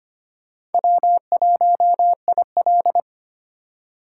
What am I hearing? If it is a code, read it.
Morse code, heard as a keyed tone, about 32 wpm, 700 Hz. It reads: W1IL